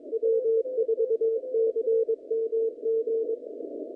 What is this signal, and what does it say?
Morse code, heard as a keyed tone, about 22 wpm, 450 Hz. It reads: W4CMG
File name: contests / naqp-cw-k5tr-op-k5pi- -0300